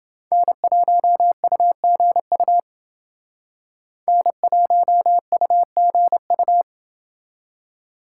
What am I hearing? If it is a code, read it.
Morse code, heard as a keyed tone, about 30 wpm, 700 Hz. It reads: N1UGU N1UGU